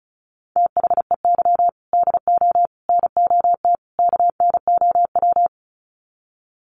Morse code, heard as a keyed tone, about 34 wpm, 700 Hz. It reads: THEY DO NOT KNOW
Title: Sets of 4 Words 35wpm